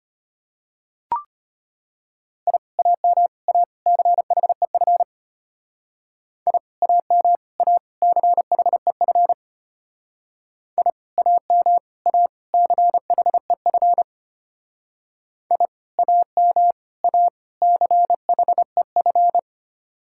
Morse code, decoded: I AM A CHEF I AM A CHEF I AM A CHEF I AM A CHEF

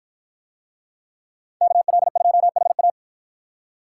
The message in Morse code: KL2HA

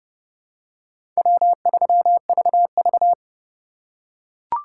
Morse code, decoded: W3VV

